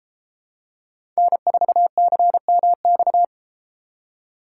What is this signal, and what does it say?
Morse code, heard as a keyed tone, about 31 wpm, 700 Hz. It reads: N4CMX